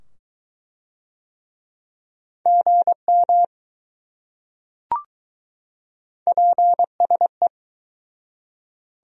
Morse code, heard as a keyed tone, about 21 wpm, 700 Hz. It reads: GM PSE